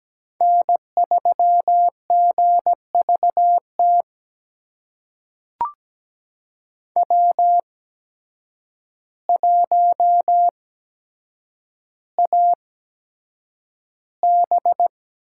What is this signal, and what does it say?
Morse code, heard as a keyed tone, about 17 wpm, spaced as for 8 wpm, 700 Hz. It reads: N3GVT W1AB
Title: USA Call Signs - 5 Characters - 8x Character Spacing - 17wpm